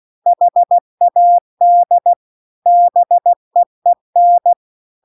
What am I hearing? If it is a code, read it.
Morse code, heard as a keyed tone, about 16 wpm, 700 Hz. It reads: HAD BEEN
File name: CW-The-Big-Twoheaded-River-2-16WPM